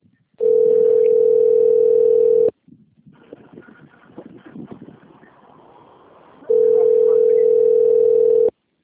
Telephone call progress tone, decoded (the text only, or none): ringback tone